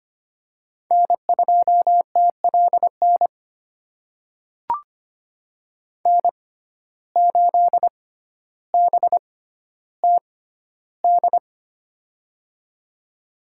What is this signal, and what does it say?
Morse code, heard as a keyed tone, about 25 wpm, spaced as for 13 wpm, 700 Hz. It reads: N2TLN N8BTD
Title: USA Call Signs - 5 Characters - 6x Character Spacing - 25wpm